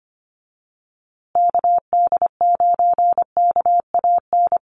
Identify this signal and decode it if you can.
Morse code, heard as a keyed tone, about 25 wpm, 700 Hz. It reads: KD9KAN
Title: USA Call Signs Repeated 3x 25wpm